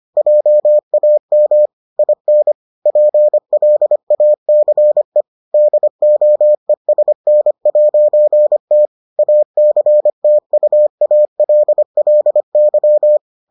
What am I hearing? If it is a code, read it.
Morse code, heard as a keyed tone, about 25 wpm, 600 Hz. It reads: JAM IN PLACE DOESN'T ACTUALLY